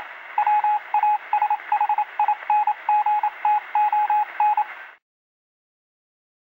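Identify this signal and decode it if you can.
Morse code, heard as a keyed tone, about 27 wpm, 840 Hz, with radio noise. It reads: WASHINGTON
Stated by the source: Morse code receive sound effect.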